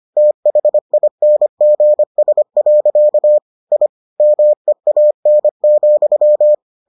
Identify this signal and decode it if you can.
Morse code, heard as a keyed tone, about 25 wpm, 600 Hz. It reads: THINGS. I MEAN,